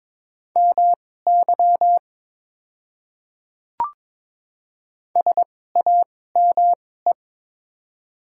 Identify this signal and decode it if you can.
Morse code, heard as a keyed tone, about 22 wpm, spaced as for 16 wpm, 700 Hz. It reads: MY SAME
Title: 2x Character Spacing - Top 300 Words - 22wpm